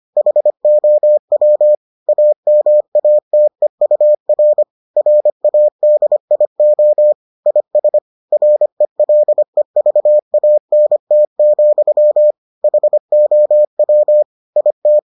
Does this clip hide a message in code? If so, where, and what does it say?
Yes: Morse code from 0.2 to 15.0 s, heard as a keyed tone, about 25 wpm, 600 Hz. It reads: HOW AMATEUR RADIO IS RELEVANT, HOW IT